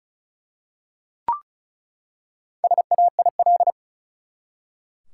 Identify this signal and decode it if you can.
Morse code, heard as a keyed tone, about 36 wpm, 700 Hz. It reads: SAIL
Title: Four Letter Words - 35wpm